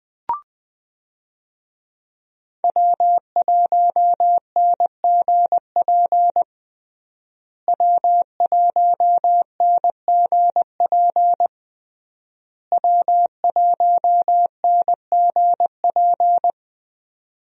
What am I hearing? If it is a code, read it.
Morse code, heard as a keyed tone, about 20 wpm, 700 Hz. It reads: W1NGP W1NGP W1NGP
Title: USA Call Signs - 5 Characters - Repeated 3x - 20wpm